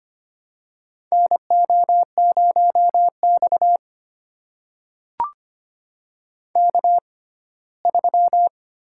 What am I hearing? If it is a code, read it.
Morse code, heard as a keyed tone, about 25 wpm, spaced as for 17 wpm, 700 Hz. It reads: NO0X K3
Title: USA Call Signs - 4 Characters - 6x Character Spacing - 25wpm